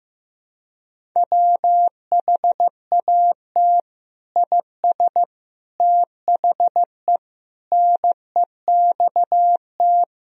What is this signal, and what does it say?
Morse code, heard as a keyed tone, about 15 wpm, 700 Hz. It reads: WHAT IS THE NEXT
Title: Sentences from Top 1000 Words - Review Part 1 - 15wpm